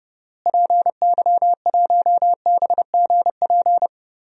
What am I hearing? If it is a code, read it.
Morse code, heard as a keyed tone, about 30 wpm, 700 Hz. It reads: PY1BGP
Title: DXCC-100 Call Signs - 6 Characters - Repeated 3x - 30wpm